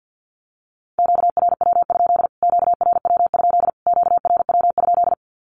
Morse code, heard as a keyed tone, about 50 wpm, 700 Hz. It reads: QRW? QRW? QRW?
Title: Q-Codes Repeated 3x 50wpm